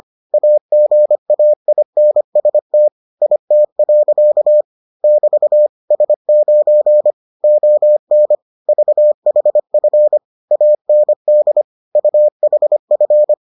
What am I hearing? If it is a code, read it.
Morse code, heard as a keyed tone, about 25 wpm, 600 Hz. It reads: AGAINST IT. <BT> S9 ON VHF AND UHF